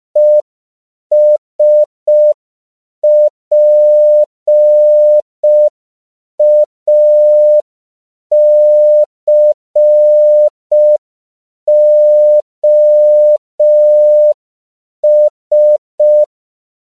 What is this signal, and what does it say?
Morse code, heard as a keyed tone, about 5 wpm, 600 Hz. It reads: ESPACOS
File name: CW / 5PPM